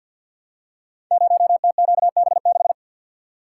Morse code, heard as a keyed tone, about 52 wpm, 700 Hz. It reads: 0TQB6
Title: Code Groups - 5 Letter:Numbers - 50wpm